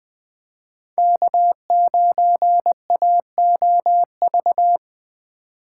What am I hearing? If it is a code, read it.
Morse code, heard as a keyed tone, about 20 wpm, 700 Hz. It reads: K9AOV
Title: USA Call Signs - 5 Characters - 4x Character Spacing - 20wpm